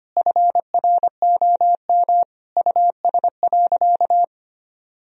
Morse code, heard as a keyed tone, about 25 wpm, 700 Hz. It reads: FROM US.
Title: Sentences from Top 1000 Words - Review Part 1 - 25wpm